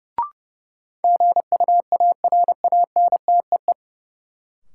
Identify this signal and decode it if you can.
Morse code, heard as a keyed tone, about 30 wpm, 700 Hz. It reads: GUARANTEE
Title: Sentences from Top 1600 Words - New Word Drill Alt - 30wpm